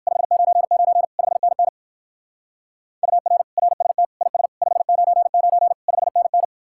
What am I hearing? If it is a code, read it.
Morse code, heard as a keyed tone, about 53 wpm, 700 Hz. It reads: 599 5NN UR RST IS 599 5NN
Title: Speed-Racing Common QSO Phrases - 45wpm